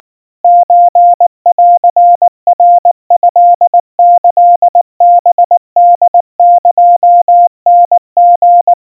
Morse code (Japanese Wabun code), heard as a keyed tone, about 19 wpm, 700 Hz. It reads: ソンナトキハホエタリ
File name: kawa / Morse sample / kamyu 70 cpm 700 Hz Crispy